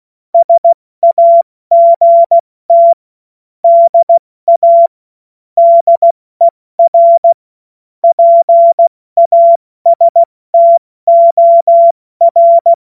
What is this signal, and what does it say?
Morse code, heard as a keyed tone, about 16 wpm, 680 Hz. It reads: SAGT DA DER PASTOR
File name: Witz-29-Juli-2025-70-F80